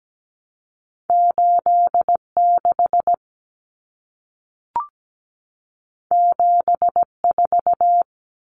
Morse code, heard as a keyed tone, about 17 wpm, 700 Hz. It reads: 86 74
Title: Common Cut Numbers - 2-Digit 17wpm